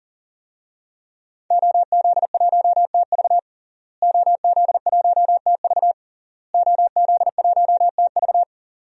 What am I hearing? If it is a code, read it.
Morse code, heard as a keyed tone, about 40 wpm, 700 Hz. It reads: OZ1TV OZ1TV OZ1TV
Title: DXCC-100 Call Signs - 5 Characters - Repeated 3x - 40wpm